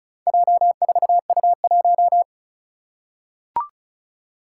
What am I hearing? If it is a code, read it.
Morse code, heard as a keyed tone, about 36 wpm, 700 Hz. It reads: J4U1
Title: Four Challenging Letter-Number Combinations 35wpm